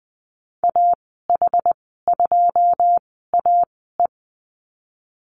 Morse code, heard as a keyed tone, about 20 wpm, spaced as for 15 wpm, 700 Hz. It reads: AH2AE
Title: 082 Lesson 14g - Number 2 - Call Signs 20wpm